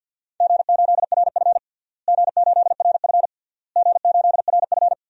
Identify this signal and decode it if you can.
Morse code, heard as a keyed tone, about 53 wpm, 700 Hz. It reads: G8RF G8RF G8RF
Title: DXCC-100 Call Signs - 4 Characters - Repeated 3x - 50wpm